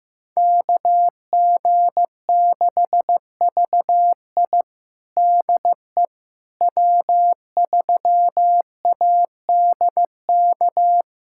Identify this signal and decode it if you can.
Morse code, heard as a keyed tone, about 15 wpm, 700 Hz. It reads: KG6VI DE W3ADK
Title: USA Callsign DE Callsign - 5 Characters - 15wpm